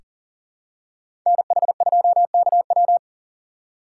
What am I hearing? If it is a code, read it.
Morse code, heard as a keyed tone, about 37 wpm, 700 Hz. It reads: NH2KW
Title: USA Call Signs - 3x Character Spacing - 40wpm